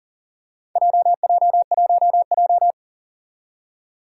Morse code, heard as a keyed tone, about 40 wpm, 700 Hz. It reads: JJ1J
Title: Four Challenging Letter-Number Combinations 40wpm